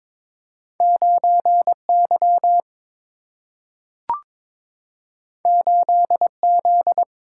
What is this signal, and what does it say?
Morse code, heard as a keyed tone, about 22 wpm, 700 Hz. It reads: 9Y 8Z
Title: Single-Letter Single-Number 22wpm